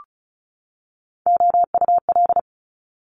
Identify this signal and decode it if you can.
Morse code, heard as a keyed tone, about 36 wpm, 700 Hz. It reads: OUL
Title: Multi-letter Phonemes 35wpm